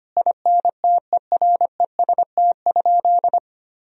Morse code, heard as a keyed tone, about 25 wpm, 700 Hz. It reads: INTEREST?
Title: Sentences from Top 1000 Words - Review Part 2 - 25wpm